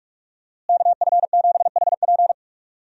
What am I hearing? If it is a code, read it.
Morse code, heard as a keyed tone, about 46 wpm, 700 Hz. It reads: KF7HP
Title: USA Call Signs - 5 Characters - Repeated 3x - 45wpm